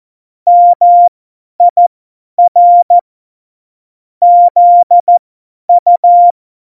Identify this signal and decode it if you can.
Morse code, heard as a keyed tone, about 14 wpm, spaced as for 10 wpm, 700 Hz. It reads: MIR ZU
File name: Morsekurs / Witze / Witz 16. Oktober 2018, 50 F70